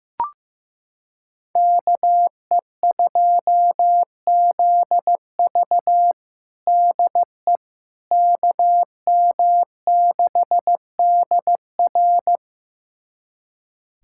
Morse code, heard as a keyed tone, about 15 wpm, 700 Hz. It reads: KE2ZV DE KM6DR
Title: USA Callsign DE Callsign - 5 Characters - 15wpm